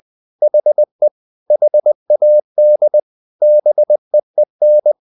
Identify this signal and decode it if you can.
Morse code, heard as a keyed tone, about 20 wpm, 600 Hz. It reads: HE HAD BEEN
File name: the old man and the sea part 2 20wpm